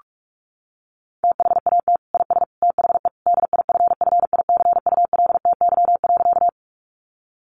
Morse code, heard as a keyed tone, about 44 wpm, 700 Hz. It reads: THAT IS THE DIFFICULTY.
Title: Sentences from Top 1500 Words 45wpm